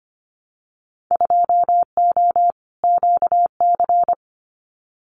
Morse code, heard as a keyed tone, about 25 wpm, 700 Hz. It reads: 2O QC
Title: Winter Field Day Exchanges 25wpm